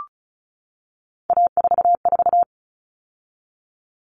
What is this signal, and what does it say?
Morse code, heard as a keyed tone, about 34 wpm, 700 Hz. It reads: A44
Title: All Cut Numbers - 3-Digit 35wpm